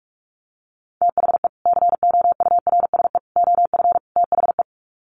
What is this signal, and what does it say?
Morse code, heard as a keyed tone, about 44 wpm, 700 Hz. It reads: THE COURSE OF THE
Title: Sets of 4 Words 45wpm